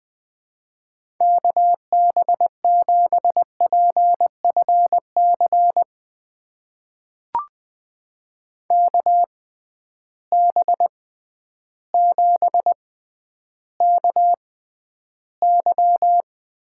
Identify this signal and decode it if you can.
Morse code, heard as a keyed tone, about 21 wpm, spaced as for 11 wpm, 700 Hz. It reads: KB7PFC KB7KY
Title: USA Call Signs - 6 Characters - 6x Character Spacing - 20wpm